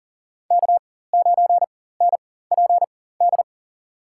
Morse code, heard as a keyed tone, about 40 wpm, spaced as for 19 wpm, 700 Hz. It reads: K9NPD
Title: USA Call Signs - 5 Characters - 4x Character Spacing - 40wpm